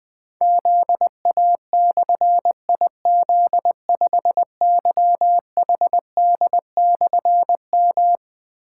Morse code, heard as a keyed tone, about 20 wpm, 700 Hz. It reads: ZA/IZ5YHD/M
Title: Difficult Call Signs 20wpm